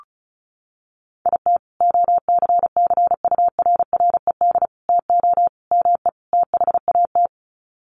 Morse code, heard as a keyed tone, about 34 wpm, 700 Hz. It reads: IT OCCURRED TO ME THAT
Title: Sets of 5 Words Encore 35wpm